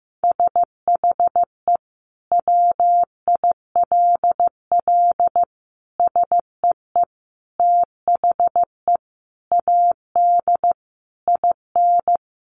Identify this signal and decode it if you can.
Morse code, heard as a keyed tone, about 15 wpm, 700 Hz. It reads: SHE WILL SEE THE AD IN
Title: Sentences from Top 1300 Words 15wpm